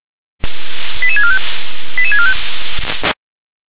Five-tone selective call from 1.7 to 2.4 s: 88645

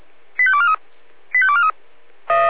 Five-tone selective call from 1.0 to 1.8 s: 85243

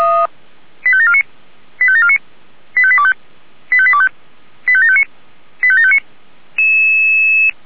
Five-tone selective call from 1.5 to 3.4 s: 85749 85825

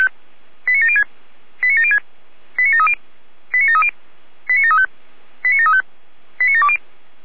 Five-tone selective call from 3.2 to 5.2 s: 89730 89735

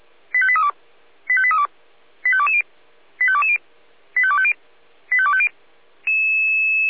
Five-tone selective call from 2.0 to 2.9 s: 85220

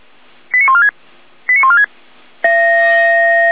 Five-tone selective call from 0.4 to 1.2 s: 89146

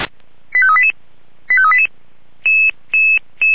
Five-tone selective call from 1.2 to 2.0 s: 85388

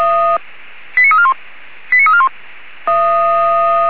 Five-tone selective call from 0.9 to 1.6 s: 89341

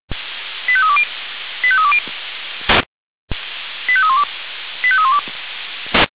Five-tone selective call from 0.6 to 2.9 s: 85320 85320

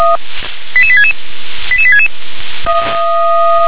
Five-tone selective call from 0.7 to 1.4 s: 88866

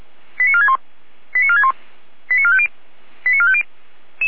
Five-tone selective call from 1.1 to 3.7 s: 89461 89450 89450

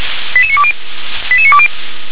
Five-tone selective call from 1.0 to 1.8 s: 88920

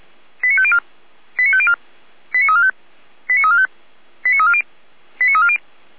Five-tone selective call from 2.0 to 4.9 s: 89345 89345 89340

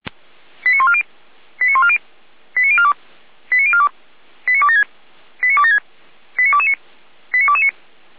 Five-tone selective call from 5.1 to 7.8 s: 89276 89229 89229

